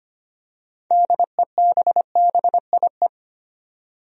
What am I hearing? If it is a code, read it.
Morse code, heard as a keyed tone, about 25 wpm, 700 Hz. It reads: DEBBIE
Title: USA Common Names - 25wpm